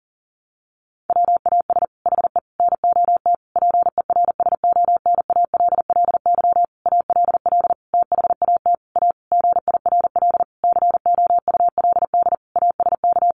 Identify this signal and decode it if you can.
Morse code, heard as a keyed tone, about 40 wpm, 700 Hz. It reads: WAS HE NOT PERSONALLY ALL THAT A GIRL COULD ASK
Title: Sentences from Top 1400 Words 40wpm